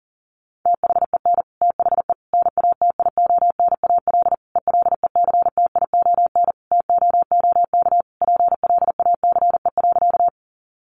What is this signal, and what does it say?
Morse code, heard as a keyed tone, about 40 wpm, 700 Hz. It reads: THEN THE NATIONAL ELECTION TOOK PLACE.
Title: Sentences from Top 1500 Words 40wpm